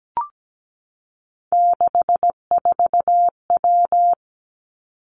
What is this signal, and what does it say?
Morse code, heard as a keyed tone, about 17 wpm, 700 Hz. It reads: 64W
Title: Three Challenging Letter-Number Combinations 17wpm